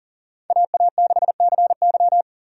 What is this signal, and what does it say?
Morse code, heard as a keyed tone, about 40 wpm, 700 Hz. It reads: AA6CY
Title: USA Call Signs - 5 Characters - Repeated 3x - 40wpm